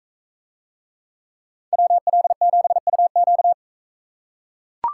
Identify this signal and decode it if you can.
Morse code, heard as a keyed tone, about 41 wpm, 700 Hz. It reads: WL7UQ